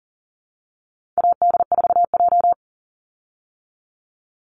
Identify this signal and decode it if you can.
Morse code, heard as a keyed tone, about 40 wpm, 700 Hz. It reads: AD4J
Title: Call Signs Repeated 3x - 40wpm